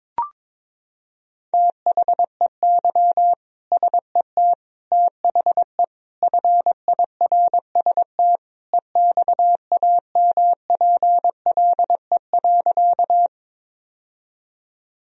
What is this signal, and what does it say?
Morse code, heard as a keyed tone, about 21 wpm, 700 Hz. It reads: THEY SET THE FIRST EXAMPLE.
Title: Sentences from Top 1000 Words - Review Part 1 - 22wpm